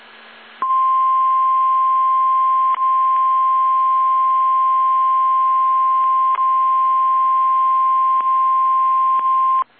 Emergency Alert System (SAME) tones (the 1050 Hz attention tone). It is heard from 0.8 to 9.6 s.
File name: eas / KSDZKDJL / KSDZKDJL